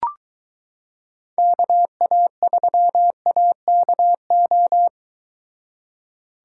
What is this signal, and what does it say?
Morse code, heard as a keyed tone, about 23 wpm, 700 Hz. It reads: KA3AKO